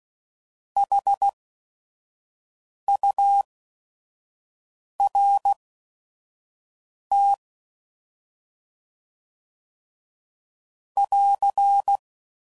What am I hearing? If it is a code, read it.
Morse code, heard as a keyed tone, about 16 wpm, spaced as for 5 wpm, 800 Hz. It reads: HURT <AR>
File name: Prefixes / By First Letter / S vs Hs